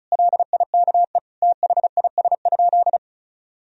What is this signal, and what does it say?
Morse code, heard as a keyed tone, about 34 wpm, 700 Hz. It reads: LIKE THIS?
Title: Sentences from Top 1000 Words - Review Part 2 - 35wpm